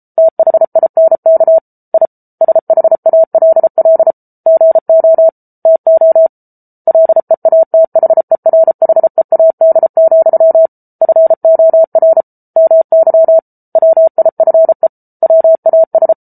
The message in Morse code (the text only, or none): THINK I SHALL GO TO LEATHERHEAD, FOR MY WIFE WAS